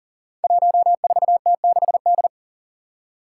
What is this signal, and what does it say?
Morse code, heard as a keyed tone, about 40 wpm, 700 Hz. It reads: 14T6D
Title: Five Challenging Letter-Number Combinations 40wpm